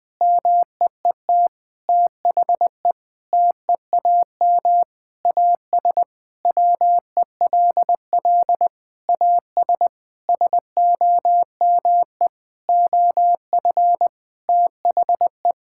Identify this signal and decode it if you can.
Morse code, heard as a keyed tone, about 21 wpm, 700 Hz. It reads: MEET THE TEAM AS WELL AS SOME OF THE
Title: Sentences with 6-Word Phrases - 20wpm